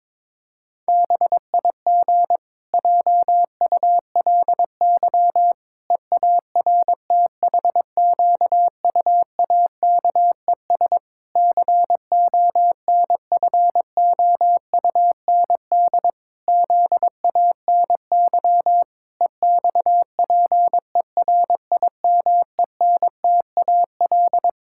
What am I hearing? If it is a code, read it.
Morse code, heard as a keyed tone, about 22 wpm, 700 Hz. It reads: BIG JULY EARTHQUAKES CONFOUND ZANY EXPERIMENTAL
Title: Pangram Challenge v2 - 22wpm